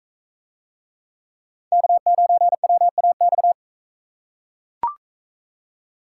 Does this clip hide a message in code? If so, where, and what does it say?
Yes: Morse code from 1.7 to 3.5 s, heard as a keyed tone, about 41 wpm, 700 Hz. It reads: K9WAX